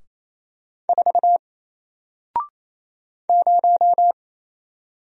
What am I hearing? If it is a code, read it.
Morse code, heard as a keyed tone, about 27 wpm, 700 Hz. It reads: 4 0
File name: Single-Letter-Number-28wpm